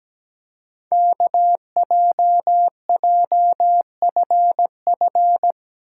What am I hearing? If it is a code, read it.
Morse code, heard as a keyed tone, about 17 wpm, 700 Hz. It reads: KJJFF
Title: Code Groups - 5 Letters - 17wpm